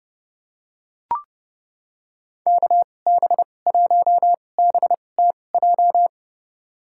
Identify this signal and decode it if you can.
Morse code, heard as a keyed tone, about 31 wpm, spaced as for 22 wpm, 700 Hz. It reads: KB1BTJ
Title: USA Call Signs - 6 Characters - 2x Character Spacing - 30wpm